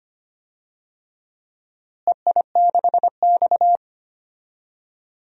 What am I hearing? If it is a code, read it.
Morse code, heard as a keyed tone, about 25 wpm, 700 Hz. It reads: EI6X
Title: DXCC-100 Call Signs - 4 Characters - Repeated 3x - 25wpm